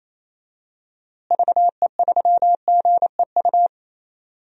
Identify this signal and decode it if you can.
Morse code, heard as a keyed tone, about 27 wpm, 700 Hz. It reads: VE3GEU
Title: DXCC-100 Call Signs - 6 Characters - Repeated 3x - 28wpm